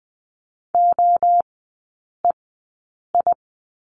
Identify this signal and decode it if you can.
Morse code, heard as a keyed tone, about 20 wpm, spaced as for 8 wpm, 700 Hz. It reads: OEI
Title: 041 Lesson 8e - Number 1 - Three Characters 20wpm